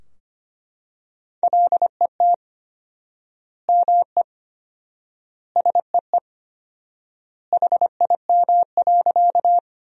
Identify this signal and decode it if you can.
Morse code, heard as a keyed tone, about 25 wpm, 700 Hz. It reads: LET ME SEE HIM.
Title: 4x Word-Spacing - Sentences from Top 300 Words - 25wpm